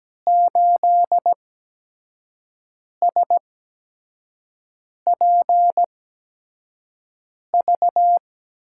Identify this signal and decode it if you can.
Morse code, heard as a keyed tone, about 17 wpm, spaced as for 5 wpm, 700 Hz. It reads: 8SPV
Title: USA Call Signs - 6 Characters - 8x Character Spacing - 17wpm